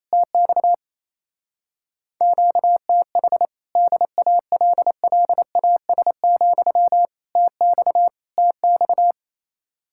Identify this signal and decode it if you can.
Morse code, heard as a keyed tone, about 32 wpm, 700 Hz. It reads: TX QTH DALLAS, TX TX
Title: Speed-Racing Common QSO Phrases - 28wpm